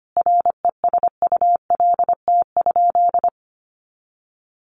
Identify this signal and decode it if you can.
Morse code, heard as a keyed tone, about 25 wpm, 700 Hz. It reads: RESULT?
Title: Sentences from Top 1500 Words 25wpm